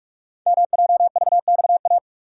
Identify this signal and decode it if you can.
Morse code, heard as a keyed tone, about 44 wpm, 700 Hz. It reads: MJVXA
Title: Code Groups - 5 Letters - 45wpm